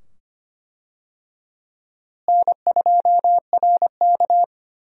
Morse code, heard as a keyed tone, about 25 wpm, 700 Hz. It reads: N2RK
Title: USA Call Signs - 4 Characters - Repeated 2x - 25wpm